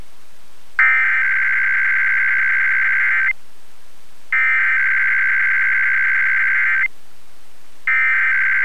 Emergency Alert System (SAME) tones (header bursts), heard as harsh buzzing bursts. It is heard from 0.8 to 7.7 s.